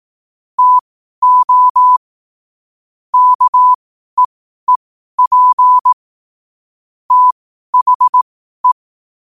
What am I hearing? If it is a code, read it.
Morse code, heard as a keyed tone, about 18 wpm, spaced as for 12 wpm, 1000 Hz. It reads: TO KEEP THE
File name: qotdfiles / 12wpm